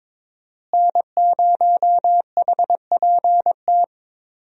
Morse code, heard as a keyed tone, about 22 wpm, 700 Hz. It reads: N0HPT
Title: USA Call Signs - 5 Characters - Repeated 3x - 22wpm